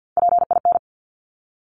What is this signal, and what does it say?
Morse code, heard as a keyed tone, about 50 wpm, 700 Hz. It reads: PIN
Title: Top 1751-2000 Words 50wpm